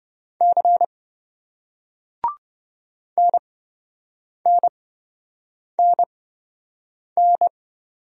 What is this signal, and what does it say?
Morse code, heard as a keyed tone, about 26 wpm, spaced as for 7 wpm, 700 Hz. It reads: C NNNN